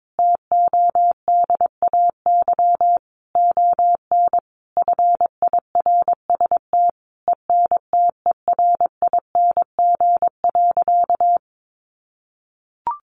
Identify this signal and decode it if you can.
Morse code, heard as a keyed tone, about 22 wpm, 700 Hz. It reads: TODAY ON FIRST ENTERING.
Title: A Sound Of Thunder 22wpm